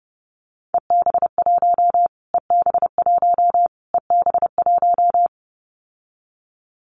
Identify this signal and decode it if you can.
Morse code, heard as a keyed tone, about 30 wpm, 700 Hz. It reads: EB1 EB1 EB1
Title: Callsign Prefixes for Top 25 DXCC - Repeated 3x - 30wpm